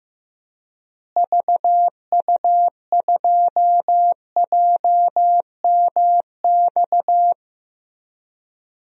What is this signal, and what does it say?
Morse code, heard as a keyed tone, about 15 wpm, 700 Hz. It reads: VU2JMX
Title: DXCC-100 Call Signs - 6 Characters - Repeated 3x - 15wpm